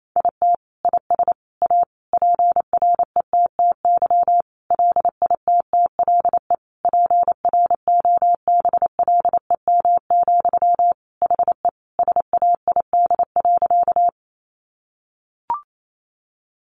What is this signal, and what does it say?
Morse code, heard as a keyed tone, about 28 wpm, 700 Hz. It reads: IT IS A PRETTY LITTLE PROBLEM, HE SAID.
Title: Sentences from Top 1000 Words 28wpm